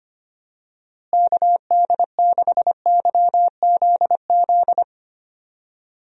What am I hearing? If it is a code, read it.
Morse code, heard as a keyed tone, about 25 wpm, 700 Hz. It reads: KD6YZZ
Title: USA Call Signs - 6 Characters - Repeated 3x - 25wpm